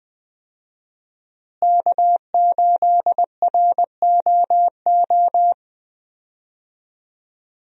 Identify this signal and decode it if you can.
Morse code, heard as a keyed tone, about 20 wpm, 700 Hz. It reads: K8ROO
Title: USA Call Signs - 5 Characters - Repeated 3x - 20wpm